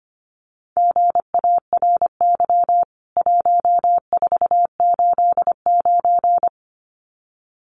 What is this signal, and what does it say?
Morse code, heard as a keyed tone, about 25 wpm, 700 Hz. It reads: GARY 1489
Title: CWT Exchange 25wpm